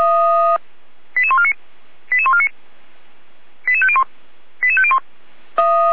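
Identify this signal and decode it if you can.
Five-tone selective call (ZVEI), heard as quick runs of five tones, tones of about 70 ms. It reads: 88149 88149 88501 88501
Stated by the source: Five-tone ZVEI selective call 88399, received by SDR, fire brigade Orlamünde.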